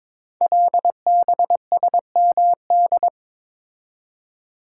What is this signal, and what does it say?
Morse code, heard as a keyed tone, about 22 wpm, 700 Hz. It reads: LBSMD
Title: Code Groups - 5 Letters - 22wpm